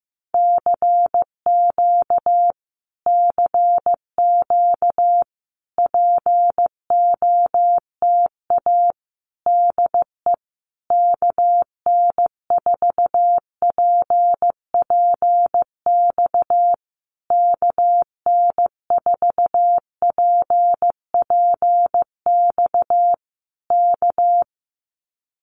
Morse code, heard as a keyed tone, about 15 wpm, 700 Hz. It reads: CQ CQ POTA DE KN4PPX KN4PPX K
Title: Common QSO Phrases Encore 15wpm